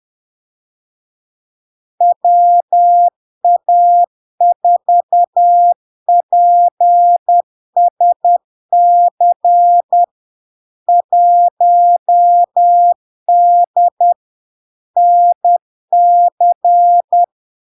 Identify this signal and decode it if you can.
Morse code, heard as a keyed tone, about 10 wpm, 700 Hz. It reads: WA4PSC 1D NC
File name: LARK-FD-2022-Log-in-CW-10wpm-700-Hz